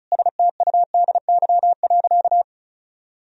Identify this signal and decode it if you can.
Morse code, heard as a keyed tone, about 36 wpm, 700 Hz. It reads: STUDY.
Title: Sentences from Top 1000 Words - Review Part 1 - 35wpm